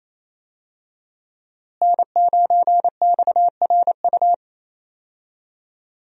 Morse code, heard as a keyed tone, about 27 wpm, 700 Hz. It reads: N9XRU
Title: USA Call Signs - 5 Characters - Repeated 3x - 28wpm